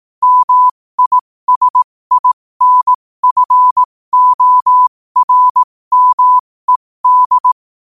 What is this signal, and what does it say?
Morse code, heard as a keyed tone, about 18 wpm, spaced as for 15 wpm, 1000 Hz. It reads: MISINFORMED
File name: qotdfiles / 15wpm